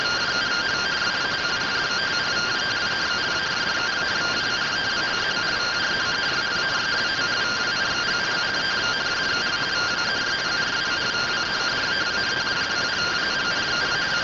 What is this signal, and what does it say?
Two-tone teleprinter signal, about 170 Hz shift.
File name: findings / MR AXIAL fr1 120.gen-02 rtty rx